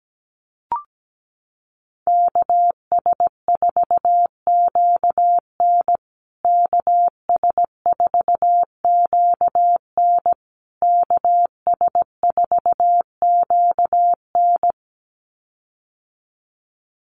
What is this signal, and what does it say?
Morse code, heard as a keyed tone, about 17 wpm, 700 Hz. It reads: KS4QN KS4QN KS4QN
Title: USA Call Signs Repeated 3x 17wpm